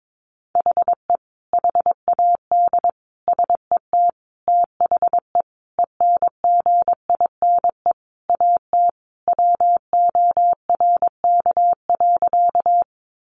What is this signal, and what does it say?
Morse code, heard as a keyed tone, about 22 wpm, 700 Hz. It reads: HE HAD SET THE ENGINE AT WORK.
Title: Sentences from Top 1500 Words 22wpm